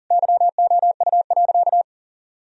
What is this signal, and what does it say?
Morse code, heard as a keyed tone, about 40 wpm, 700 Hz. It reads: YOU.
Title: Sentences from Top 1000 Words - Review Part 2 - 40wpm